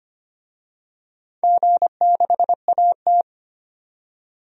Morse code, heard as a keyed tone, about 25 wpm, 700 Hz. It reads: G6AT
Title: DXCC-100 Call Signs - 4 Characters - 25wpm